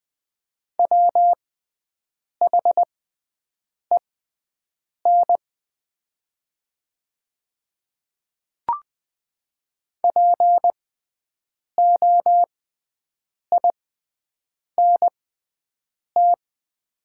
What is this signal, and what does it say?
Morse code, heard as a keyed tone, about 21 wpm, spaced as for 7 wpm, 700 Hz. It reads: WHEN POINT
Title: 6x Character Spacing - Top 300 Words - 20wpm